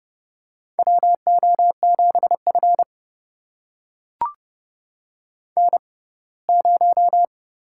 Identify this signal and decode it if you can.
Morse code, heard as a keyed tone, about 30 wpm, spaced as for 20 wpm, 700 Hz. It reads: WO7F N0
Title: USA Call Signs - 6x Character Spacing - 30wpm